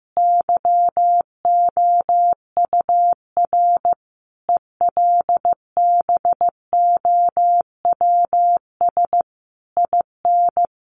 Morse code, heard as a keyed tone, about 15 wpm, 700 Hz. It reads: YOUR ELBOWS IN
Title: A Sound Of Thunder 15wpm